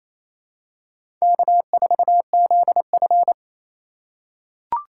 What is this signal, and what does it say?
Morse code, heard as a keyed tone, about 28 wpm, 700 Hz. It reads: K4ZF
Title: USA Call Signs - 4 Characters - 4x Character Spacing - 28wpm